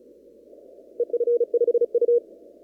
Morse code, heard as a keyed tone, about 32 wpm, 480 Hz. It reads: EF5U
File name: CW / kt5j wpx cw-op-k5pi